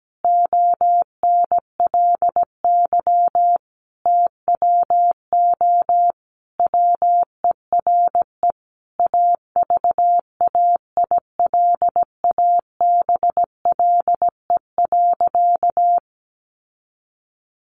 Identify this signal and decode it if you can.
Morse code, heard as a keyed tone, about 17 wpm, 700 Hz. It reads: ONLY TWO WERE AVAILABLE.
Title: Sentences from Top 1100 Words 17wpm